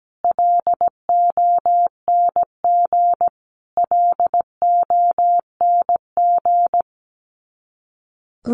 Morse code, heard as a keyed tone, about 17 wpm, 700 Hz. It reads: LONG LONG
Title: Top 200 Words Repeated 17wpm